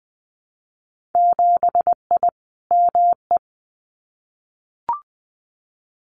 Morse code, heard as a keyed tone, about 20 wpm, 700 Hz. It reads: 7I ME
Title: Winter Field Day Exchanges Encore 20wpm